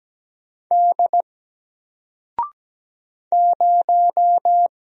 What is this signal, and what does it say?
Morse code, heard as a keyed tone, about 17 wpm, 700 Hz. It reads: D 0
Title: Single Letter-Number Alt 17wpm